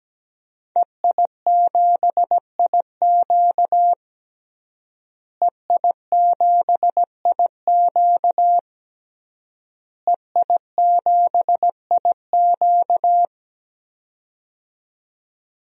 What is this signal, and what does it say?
Morse code, heard as a keyed tone, about 17 wpm, 700 Hz. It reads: EI7IQ EI7IQ EI7IQ
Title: DXCC-100 Call Signs - 5 Characters - Repeated 3x - 17wpm